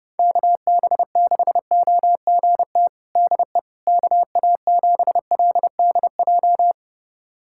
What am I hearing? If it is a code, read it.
Morse code, heard as a keyed tone, about 30 wpm, 700 Hz. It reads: KB6OGT DE KA7LDJ
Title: USA Callsign DE Callsign - 6 Characters - 30wpm